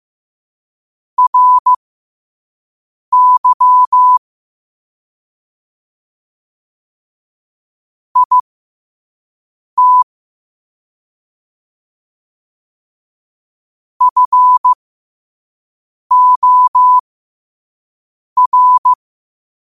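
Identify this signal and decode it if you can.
Morse code, heard as a keyed tone, about 15 wpm, spaced as for 5 wpm, 1000 Hz. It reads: RY IT FOR
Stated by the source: Morse quote-of-the-day practice at 5 wpm.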